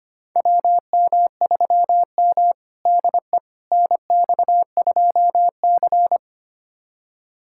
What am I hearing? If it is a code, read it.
Morse code, heard as a keyed tone, about 25 wpm, 700 Hz. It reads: WM3M DE NX2C
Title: USA Callsign DE Callsign - 4 Characters - 25wpm